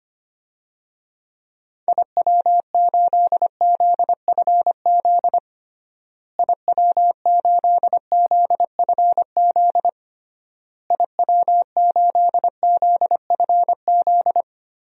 Morse code, heard as a keyed tone, about 25 wpm, 700 Hz. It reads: IW8ZFZ IW8ZFZ IW8ZFZ
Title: DXCC-100 Call Signs - 6 Characters - Repeated 3x - 25wpm